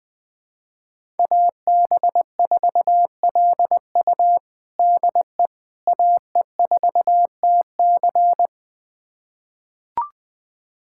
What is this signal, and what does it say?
Morse code, heard as a keyed tone, about 20 wpm, 700 Hz. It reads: AB4LU DE AE4TC
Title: USA Callsign DE Callsign - 5 Characters - 20wpm